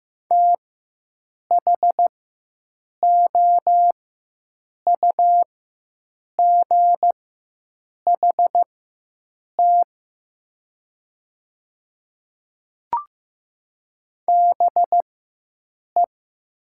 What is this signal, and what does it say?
Morse code, heard as a keyed tone, about 15 wpm, spaced as for 7 wpm, 700 Hz. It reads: THOUGHT BE
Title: 4x Character Spacing - Top 300 Words - 15wpm